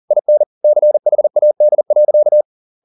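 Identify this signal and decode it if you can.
Morse code, heard as a keyed tone, about 40 wpm, 600 Hz. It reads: IN CHAD.
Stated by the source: Morse code at 40 wpm, practice text, CW med Gnister.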